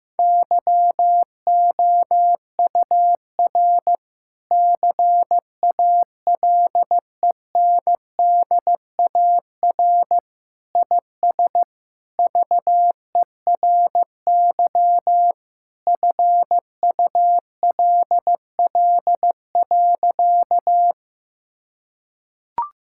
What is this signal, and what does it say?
Morse code, heard as a keyed tone, about 16 wpm, 700 Hz. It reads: YOUR CALENDAR IS VERY FULL.
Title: Sentences from Top 1600 Words 15wpm